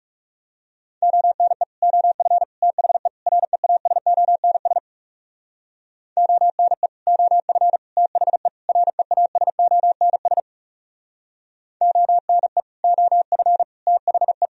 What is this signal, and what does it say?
Morse code, heard as a keyed tone, about 37 wpm, 700 Hz. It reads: ONE OF THE REASONS ONE OF THE REASONS ONE OF THE
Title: Speed-Racing - Sets of 4 Words 30wpm